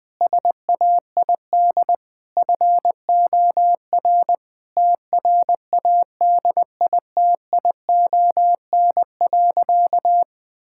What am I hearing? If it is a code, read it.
Morse code, heard as a keyed tone, about 20 wpm, 700 Hz. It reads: SAID FOR TRADITION.
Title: Sentences with 7-Word Phrases - Spoken Beginning - 20wpm